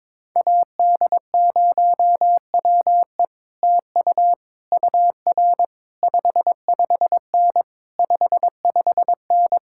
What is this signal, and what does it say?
Morse code, heard as a keyed tone, about 22 wpm, 700 Hz. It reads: AD0WE TU UR 55N 55N
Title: Speed-Racing Common QSO Phrases - 22wpm